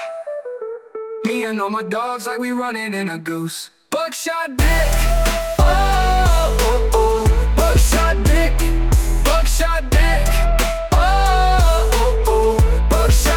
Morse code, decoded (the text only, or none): none